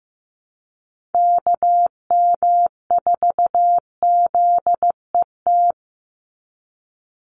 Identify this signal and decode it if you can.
Morse code, heard as a keyed tone, about 15 wpm, 700 Hz. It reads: KM4ZET
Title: USA Call Signs 15wpm